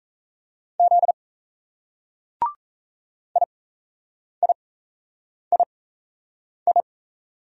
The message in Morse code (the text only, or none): Z IIII